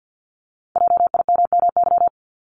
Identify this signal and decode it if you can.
Morse code, heard as a keyed tone, about 50 wpm, 700 Hz. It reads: JIMMY
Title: Multi-letter Phonemes 50wpm